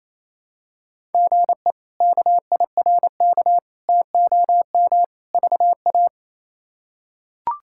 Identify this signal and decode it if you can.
Morse code, heard as a keyed tone, about 28 wpm, 700 Hz. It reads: GE KIRK TOM VA
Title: K1USN SST QSO Practice - 28wpm